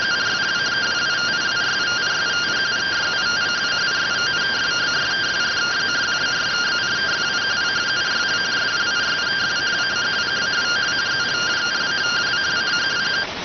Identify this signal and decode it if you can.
Two-tone teleprinter signal, about 170 Hz shift.